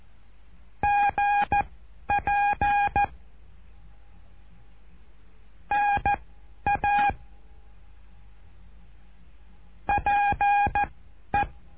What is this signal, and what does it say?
Morse code, heard as a keyed tone, about 14 wpm, spaced as for 10 wpm, 820 Hz. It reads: GP NA PE
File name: cursos / CURSO DE TELEGRAFIA / Telegrafia - Trilha 08